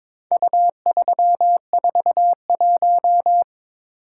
Morse code, heard as a keyed tone, about 22 wpm, 700 Hz. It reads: U341
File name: practice / 22wpm / Four Letter-Number Combinations - 22wpm